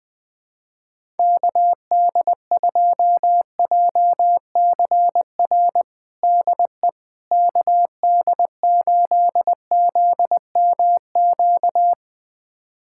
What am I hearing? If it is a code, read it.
Morse code, heard as a keyed tone, about 20 wpm, 690 Hz. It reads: KD2JCR DE KD8ZMQ
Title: USA Callsign DE Callsign 20wpm